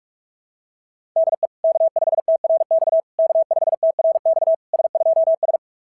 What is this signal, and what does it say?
Morse code, heard as a keyed tone, about 44 wpm, 690 Hz. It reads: DE K5TRX K5TRX S2S
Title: SOTA QSO Practice 45wpm